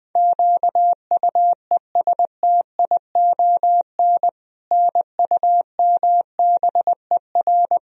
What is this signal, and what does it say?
Morse code, heard as a keyed tone, about 20 wpm, 700 Hz. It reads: QUESTION NUMBER
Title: Sentences from Top 1000 Words - Review Part 1 - 20wpm